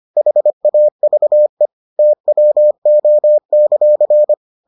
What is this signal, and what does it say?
Morse code, heard as a keyed tone, about 25 wpm, 600 Hz. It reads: HAVE TWO;